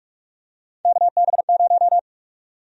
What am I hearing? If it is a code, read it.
Morse code, heard as a keyed tone, about 46 wpm, 700 Hz. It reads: KB0
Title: USA Call Sign Prefixes - 45wpm